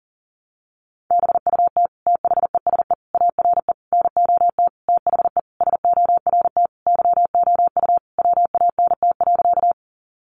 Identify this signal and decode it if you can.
Morse code, heard as a keyed tone, about 40 wpm, 700 Hz. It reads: BUT THESE ARE NOT THE SORT YOU WANT.
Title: Sentences from Top 1300 Words 40wpm